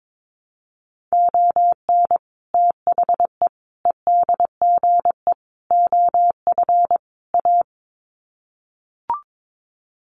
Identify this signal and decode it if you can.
Morse code, heard as a keyed tone, about 22 wpm, 700 Hz. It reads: ON THE EDGE OF A
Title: Sets of 5 Words Encore2 22wpm